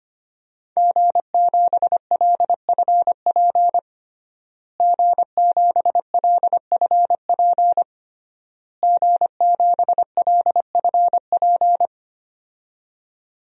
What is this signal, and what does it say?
Morse code, heard as a keyed tone, about 25 wpm, 700 Hz. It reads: G7LFP G7LFP G7LFP
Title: DXCC-100 Call Signs - 5 Characters - Repeated 3x - 25wpm